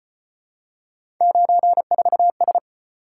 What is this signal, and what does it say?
Morse code, heard as a keyed tone, about 33 wpm, 700 Hz. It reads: 94S